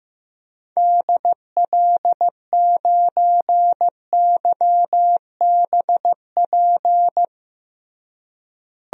Morse code, heard as a keyed tone, about 16 wpm, 700 Hz. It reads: DL9YBP
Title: DXCC-100 Call Signs - 6 Characters - Repeated 2x - 15wpm